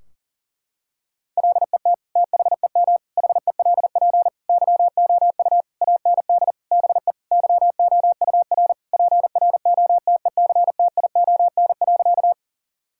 Morse code, heard as a keyed tone, about 37 wpm, 700 Hz. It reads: LET THEM HELP YOU AND BE YOUR PROTECTION.
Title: Sentences from Top 1600 Words 40wpm